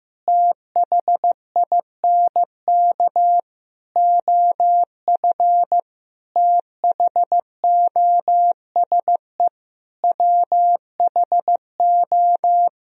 Morse code, heard as a keyed tone, about 15 wpm, 700 Hz. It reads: THINK OF THOSE WHO
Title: Sentences from Top 1000 Words - Review Part 2 - 15wpm